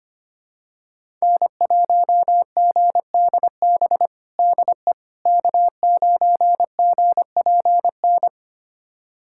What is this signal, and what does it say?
Morse code, heard as a keyed tone, about 25 wpm, 700 Hz. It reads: N1GDB DE K9GPN
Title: USA Callsign DE Callsign - 5 Characters - 25wpm